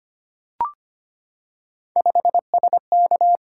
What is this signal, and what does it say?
Morse code, heard as a keyed tone, about 25 wpm, 700 Hz. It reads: 5SK
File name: Three Letter-Number Combinations - 25wpm